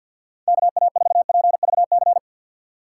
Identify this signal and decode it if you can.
Morse code, heard as a keyed tone, about 52 wpm, 700 Hz. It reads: KA4PVC
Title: USA Call Signs - 6 Characters - 6x Character Spacing - 50wpm